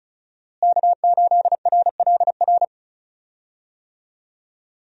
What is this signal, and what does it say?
Morse code, heard as a keyed tone, about 36 wpm, 700 Hz. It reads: K8RLR
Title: USA Call Signs - 5 Characters - Repeated 3x - 35wpm